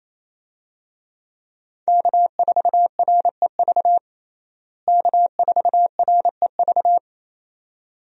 Morse code, heard as a keyed tone, about 27 wpm, 700 Hz. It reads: K4REV K4REV
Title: USA Call Signs - 5 Characters - Repeated 2x - 28wpm